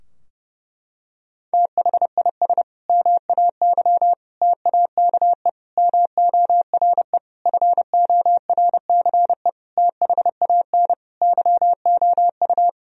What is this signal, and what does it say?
Morse code, heard as a keyed tone, about 30 wpm, 700 Hz. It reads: THIS MAY TAKE MORE FORCE THAN YOU
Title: Sentences from Top 1000 Words - Review Part 3 - 30wpm